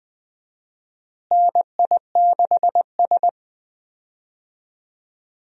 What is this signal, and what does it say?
Morse code, heard as a keyed tone, about 20 wpm, 700 Hz. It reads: NI6S
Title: USA Call Signs - 4 Characters - Repeated 3x - 20wpm